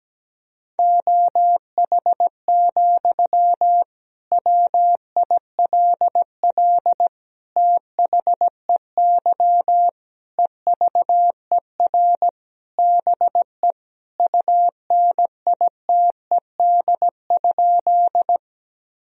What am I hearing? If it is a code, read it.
Morse code, heard as a keyed tone, about 17 wpm, 700 Hz. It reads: OH, WILL THEY EVER BE UNITED?
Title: Sentences from Top 1000 Words - Review Part 4 - 17wpm